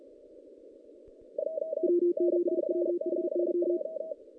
Morse code, heard as a keyed tone, about 38 wpm, 350 Hz. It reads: W7WHY
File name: 2016 JAN CW NAQP